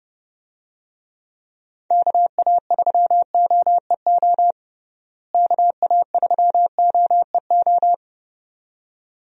Morse code, heard as a keyed tone, about 30 wpm, 700 Hz. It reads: KA3OEO KA3OEO
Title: USA Call Signs - 6 Characters - Repeated 2x - 30wpm